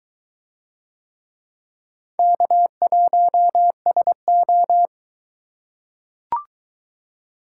Morse code, heard as a keyed tone, about 21 wpm, 700 Hz. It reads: K1SO